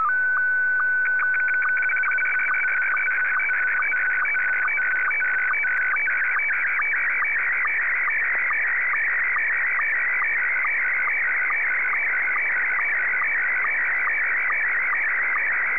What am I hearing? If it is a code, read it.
SSTV (slow-scan television), Scottie 1.